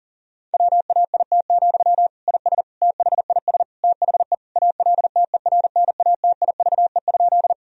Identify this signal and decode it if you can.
Morse code, heard as a keyed tone, about 37 wpm, 700 Hz. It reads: WAIT, IS THIS THE ALTERNATIVE?
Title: Sentences from Top 1000 Words - Review Part 4 - 40wpm